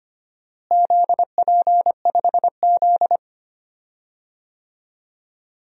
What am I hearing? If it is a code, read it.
Morse code, heard as a keyed tone, about 25 wpm, 700 Hz. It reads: ZP5Z
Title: DXCC-100 Call Signs - 4 Characters - Repeated 2x - 25wpm